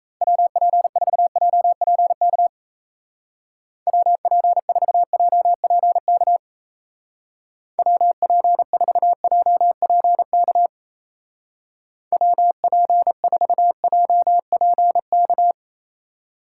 Morse code, heard as a keyed tone, about 34 wpm, 700 Hz. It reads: WP4JPK WP4JPK WP4JPK WP4JPK
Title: Speed Racing - USA 2x3 Call Signs 28wpm